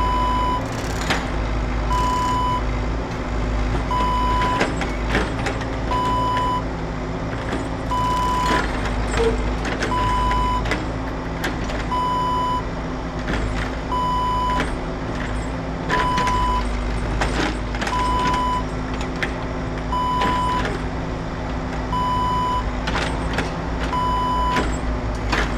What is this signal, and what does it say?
There is no code.